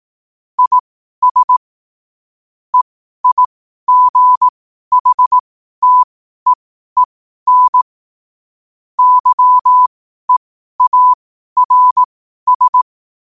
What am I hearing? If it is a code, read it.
Morse code, heard as a keyed tone, about 18 wpm, spaced as for 12 wpm, 1000 Hz. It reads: IS EIGHTEEN YEARS
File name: qotdfiles / 12wpm